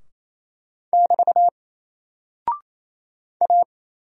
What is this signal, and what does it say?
Morse code, heard as a keyed tone, about 28 wpm, 700 Hz. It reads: <BT> A
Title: Single Character (Letter:Number:Punctuation:Prosign) Alt 28wpm